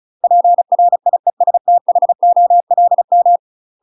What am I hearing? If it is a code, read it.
Morse code, heard as a keyed tone, about 35 wpm, 700 Hz. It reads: PRIESTHOLM
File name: UK MASTER QSOs TRAINING / 35WPM / UK MASTER QSOs 0016